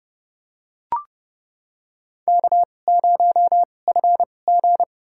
Morse code, heard as a keyed tone, about 30 wpm, spaced as for 22 wpm, 700 Hz. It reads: K0FG
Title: USA Call Signs - 4 Characters - 2x Character Spacing - 30wpm